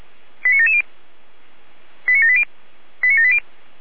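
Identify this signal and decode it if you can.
Five-tone selective call (ZVEI), heard as quick runs of five tones, tones of about 70 ms. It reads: 89770 89780 89780
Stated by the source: ZVEI five-tone selective call 89765 received on SDR, from the feuerwehr-orlamuende.org fire brigade audio archive.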